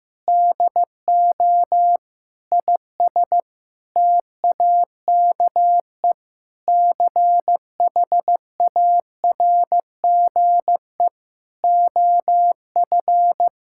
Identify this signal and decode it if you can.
Morse code, heard as a keyed tone, about 16 wpm, 700 Hz. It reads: DO IS TAKE CHARGE OF
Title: Sentences from Top 1000 Words - Review Part 6 - 15wpm